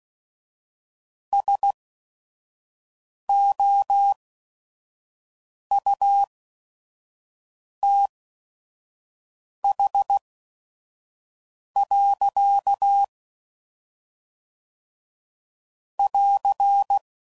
Morse code, heard as a keyed tone, about 16 wpm, spaced as for 5 wpm, 800 Hz. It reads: SOUTH. <AR>
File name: CW Code Groups @5WPM / Weather Sets / Weather 4 21